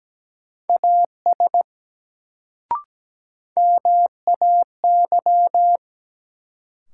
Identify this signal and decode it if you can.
Morse code, heard as a keyed tone, about 17 wpm, 700 Hz. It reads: AS MAY
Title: Sentences from Top 100 Words - New Word Drill - 17wpm